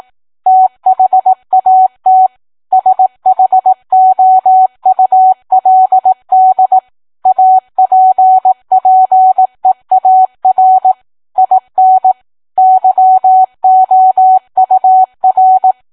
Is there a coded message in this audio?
Yes: Morse code from 0.5 to 15.7 s, heard as a keyed tone, about 18 wpm, 750 Hz. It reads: THAT SHOULD APPEAR IN YOUR